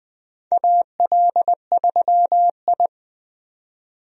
Morse code, heard as a keyed tone, about 20 wpm, 700 Hz. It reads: AL3I
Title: USA Call Signs - 4 Characters - 20wpm